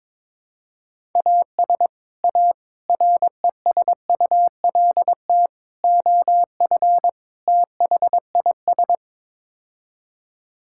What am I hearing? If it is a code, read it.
Morse code, heard as a keyed tone, about 22 wpm, 700 Hz. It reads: AS A RESULT OF THIS
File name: practice / 22wpm / Build A Phrase - 5-Word Phrases - 22wpm